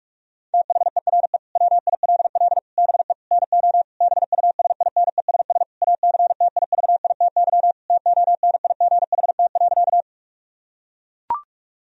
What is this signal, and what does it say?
Morse code, heard as a keyed tone, about 41 wpm, 700 Hz. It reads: THERE WILL BE NO BUSINESS ACTIVITY TONIGHT.
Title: Sentences from Top 1000 Words - Review Part 4 - 45wpm